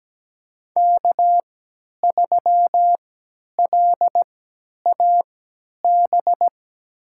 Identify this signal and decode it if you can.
Morse code, heard as a keyed tone, about 17 wpm, spaced as for 10 wpm, 700 Hz. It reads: K3LAB
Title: USA Call Signs - 5 Characters - 3x Character Spacing - 17wpm